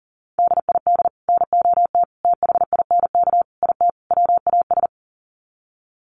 Morse code, heard as a keyed tone, about 40 wpm, 700 Hz. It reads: DID NOT THINK IT WAS
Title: Sets of 5 Words Encore 40wpm